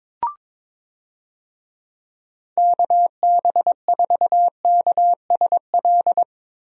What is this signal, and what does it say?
Morse code, heard as a keyed tone, about 22 wpm, 700 Hz. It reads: KB4KSL
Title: USA Call Signs - 6 Characters - Repeated 3x - 22wpm